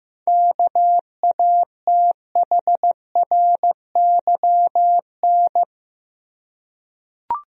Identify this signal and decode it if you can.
Morse code, heard as a keyed tone, about 15 wpm, 700 Hz. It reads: KATHRYN
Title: USA Common Names - 15wpm